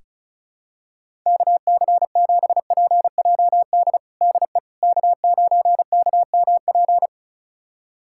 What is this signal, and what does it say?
Morse code, heard as a keyed tone, about 34 wpm, 700 Hz. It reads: KC7PJD DE K9KMP
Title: USA Callsign DE Callsign 35wpm